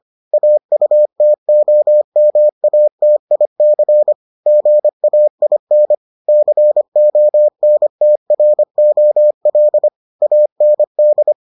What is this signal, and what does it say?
Morse code, heard as a keyed tone, about 25 wpm, 600 Hz. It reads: AUTOMATIC GAIN CONTROL AND